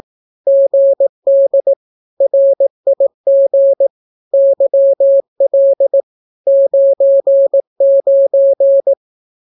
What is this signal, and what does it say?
Morse code, heard as a keyed tone, about 18 wpm, 550 Hz. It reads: GD RIG YL 99
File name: naukapodstaw / skroty / skroty